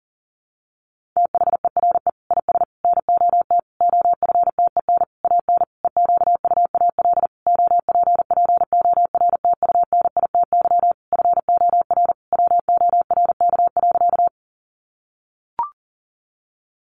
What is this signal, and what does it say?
Morse code, heard as a keyed tone, about 40 wpm, 700 Hz. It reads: THERE IS NOT OFTEN AN EQUAL OPPORTUNITY FOR WORK.
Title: Sentences from Top 1400 Words 40wpm